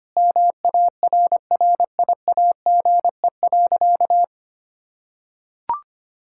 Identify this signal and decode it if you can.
Morse code, heard as a keyed tone, about 25 wpm, 700 Hz. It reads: MARRIAGE.
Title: Sentences from Top 1000 Words - Review Part 5 - 25wpm